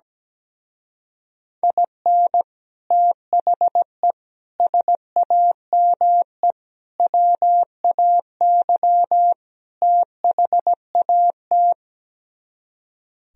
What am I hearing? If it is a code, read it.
Morse code, heard as a keyed tone, about 17 wpm, 700 Hz. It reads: IN THE SAME WAY THAT
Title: Speed-Racing - Sets of 5 Words 17wpm